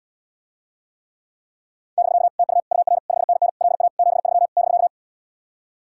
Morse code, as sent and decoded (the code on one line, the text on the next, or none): -...-.- / - ..- / ..- .-. / ..... -. -. / .-.. .- / --... ...-- / -...-.-
<BK> TU UR 5NN LA 73 <BK>